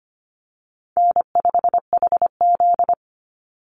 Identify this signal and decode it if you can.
Morse code, heard as a keyed tone, about 25 wpm, 700 Hz. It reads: N5HZ
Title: Easy Call Signs - 25wpm